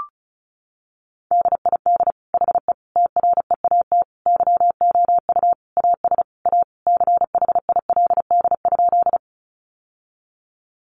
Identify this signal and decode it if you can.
Morse code, heard as a keyed tone, about 34 wpm, 700 Hz. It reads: DID HE TREAT YOU AS A CHILD?
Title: Sentences from Top 1300 Words 35wpm